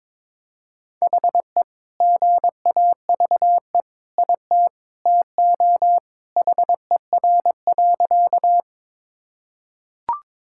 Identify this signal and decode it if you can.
Morse code, heard as a keyed tone, about 22 wpm, 700 Hz. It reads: HE GAVE IT TO HER.
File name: practice / 22wpm / Sentences with 2-Word Phrases - 22wpm